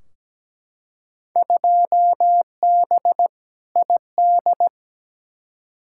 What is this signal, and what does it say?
Morse code, heard as a keyed tone, about 17 wpm, 700 Hz. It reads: 2B ID
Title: Field Day Exchanges Encore - 17wpm